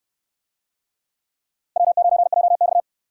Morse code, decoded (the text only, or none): W9JX